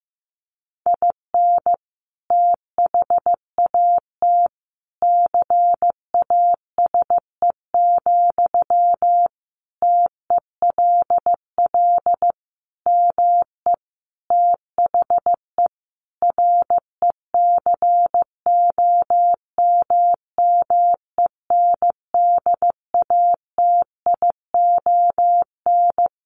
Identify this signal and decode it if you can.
Morse code, heard as a keyed tone, about 15 wpm, 700 Hz. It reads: IN THAT CASE, TELL ME THE RECOMMENDATION
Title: Sentences from Top 1400 Words 15wpm